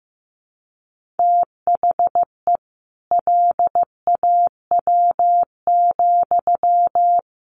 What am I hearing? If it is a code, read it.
Morse code, heard as a keyed tone, about 15 wpm, 700 Hz. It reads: THE LAW,
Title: Sentences from Top 1400 Words 15wpm